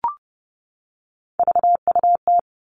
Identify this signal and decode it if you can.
Morse code, heard as a keyed tone, about 30 wpm, 700 Hz. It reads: VUT
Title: All Cut Numbers - 3-Digit 30wpm